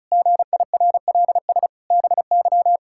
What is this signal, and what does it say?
Morse code, heard as a keyed tone, about 36 wpm, 700 Hz. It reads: GIRLS BY
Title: Pangram Challenge v2 - 35wpm